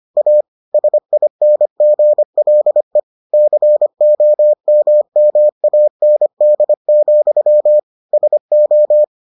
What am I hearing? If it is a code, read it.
Morse code, heard as a keyed tone, about 25 wpm, 600 Hz. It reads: A SINGLE COMMAND, SO